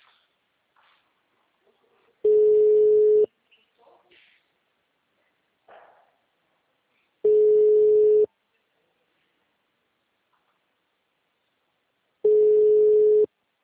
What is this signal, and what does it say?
Ringback tone (Europe, 425 Hz).
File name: out- -2003- - - .48